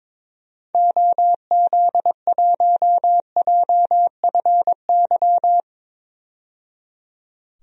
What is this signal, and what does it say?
Morse code, heard as a keyed tone, about 22 wpm, 700 Hz. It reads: OZ1JFY
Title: DXCC-100 Call Signs - 6 Characters - 22wpm